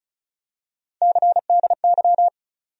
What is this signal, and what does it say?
Morse code, heard as a keyed tone, about 36 wpm, 700 Hz. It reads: CDY
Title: USA Call Sign Postfixes - 3 Letters 35wpm